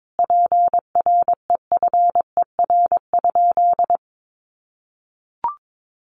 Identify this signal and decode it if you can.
Morse code, heard as a keyed tone, about 22 wpm, 700 Hz. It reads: PREFER?
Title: Sentences from Top 1100 Words 22wpm